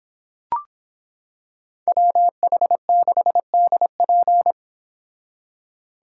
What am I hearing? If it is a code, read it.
Morse code, heard as a keyed tone, about 25 wpm, 700 Hz. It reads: WH6DP